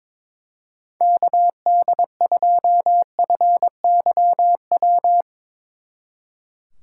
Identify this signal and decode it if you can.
Morse code, heard as a keyed tone, about 22 wpm, 700 Hz. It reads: KD2FYW
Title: Speed Racing - USA 2x3 Call Signs 22wpm